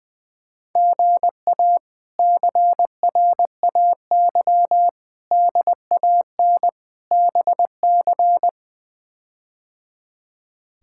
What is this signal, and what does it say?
Morse code, heard as a keyed tone, about 21 wpm, 700 Hz. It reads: GA CRAY DAN BC
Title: K1USN SST QSO Practice - 20wpm